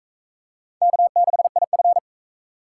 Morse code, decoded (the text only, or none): K6IF